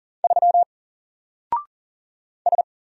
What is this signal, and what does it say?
Morse code, heard as a keyed tone, about 40 wpm, 700 Hz. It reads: 3 S
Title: Single Character (Letter:Number:Punctuation:Prosign) 40wpm